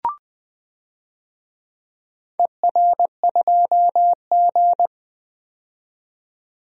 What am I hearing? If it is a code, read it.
Morse code, heard as a keyed tone, about 20 wpm, 700 Hz. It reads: ER2G
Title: DXCC-100 Call Signs - 4 Characters - 20wpm